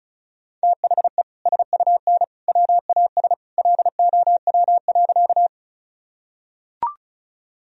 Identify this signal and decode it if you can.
Morse code, heard as a keyed tone, about 36 wpm, 690 Hz. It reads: THE SUN WAS LOW.
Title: Sentences from Top 1000 Words - Review Part 1 - 35wpm